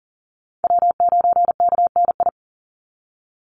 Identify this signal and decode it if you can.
Morse code, heard as a keyed tone, about 40 wpm, 700 Hz. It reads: W9KNI
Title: Call Signs - 40wpm